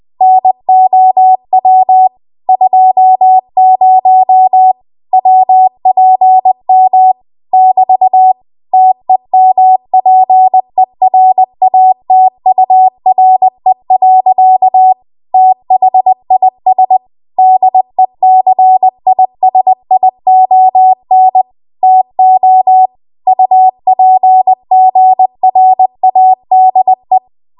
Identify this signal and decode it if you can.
Morse code, heard as a keyed tone, about 20 wpm, 750 Hz. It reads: NOW 20 WPM <BT> TEMPERATURE. THIS DECISION TO UPGRADE
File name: Archive / 20 WPM / 20WPM